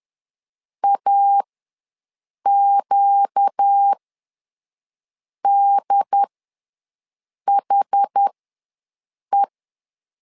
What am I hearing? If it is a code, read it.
Morse code, heard as a keyed tone, about 10 wpm, spaced as for 5 wpm, 800 Hz. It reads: AQDHE
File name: LEZIONI / 4 Q ws 10wpm 2306